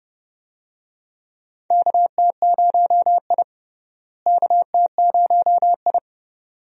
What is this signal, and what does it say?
Morse code, heard as a keyed tone, about 30 wpm, 700 Hz. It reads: KT0I KT0I
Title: USA Call Signs - 4 Characters - Repeated 2x - 30wpm